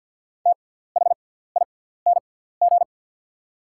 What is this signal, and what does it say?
Morse code, heard as a keyed tone, about 48 wpm, spaced as for 17 wpm, 690 Hz. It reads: THING
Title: 6x Character Spacing - Top 100 Words - 50wpm